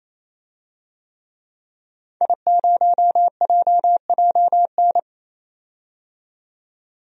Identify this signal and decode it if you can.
Morse code, heard as a keyed tone, about 27 wpm, 700 Hz. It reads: I0JJN
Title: DXCC-100 Call Signs - 5 Characters - 28wpm